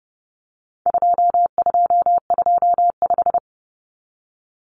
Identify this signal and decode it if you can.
Morse code, heard as a keyed tone, about 30 wpm, 700 Hz. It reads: 2225
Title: Common Cut Numbers - 4-Digit 30wpm